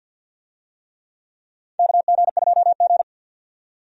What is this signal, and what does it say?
Morse code, heard as a keyed tone, about 52 wpm, 700 Hz. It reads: KG2G
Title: USA Call Signs - 4 Characters - 50wpm